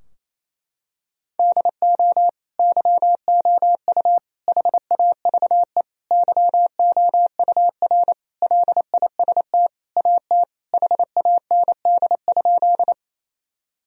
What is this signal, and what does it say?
Morse code, heard as a keyed tone, about 28 wpm, 700 Hz. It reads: DO YOU HAVE YOUR LIST AT HAND?
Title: Sentences from Top 1000 Words - Review Part 2 - 28wpm